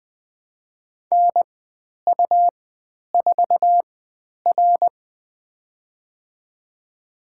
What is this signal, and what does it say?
Morse code, heard as a keyed tone, about 20 wpm, spaced as for 10 wpm, 700 Hz. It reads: NU4R
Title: 109 Lesson 18g - Letter m - Call Signs 20wpm